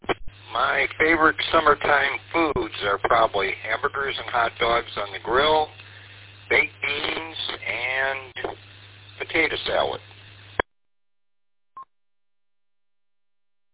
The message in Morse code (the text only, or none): none